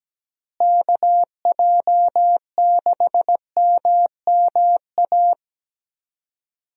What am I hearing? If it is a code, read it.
Morse code, heard as a keyed tone, about 17 wpm, 700 Hz. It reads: KJ6MMA
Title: USA Call Signs - 6 Characters - Repeated 3x - 17wpm